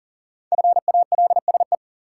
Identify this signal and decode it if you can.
Morse code, heard as a keyed tone, about 40 wpm, 700 Hz. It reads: FALSE
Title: Sentences from Top 1600 Words - New Word Drill - 40wpm